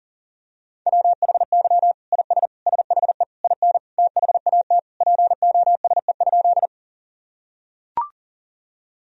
Morse code, heard as a keyed tone, about 37 wpm, 700 Hz. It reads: WHY IS SHE IN THAT POSE?
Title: Sentences from Top 1000 Words - Review Part 2 - 40wpm